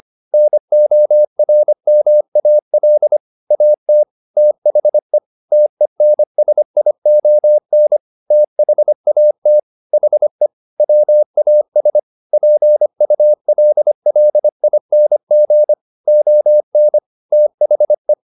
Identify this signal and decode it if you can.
Morse code, heard as a keyed tone, about 25 wpm, 600 Hz. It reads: NORMAL AT THE TENSION THAT HE WAS PULLING ON THE